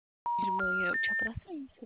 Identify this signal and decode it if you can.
Special information tone, with speech.